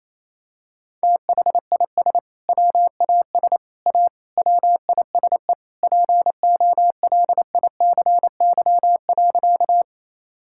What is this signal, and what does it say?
Morse code, heard as a keyed tone, about 28 wpm, 700 Hz. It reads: THIS WAS A WISE POLICY.
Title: Sentences from Top 1600 Words 28wpm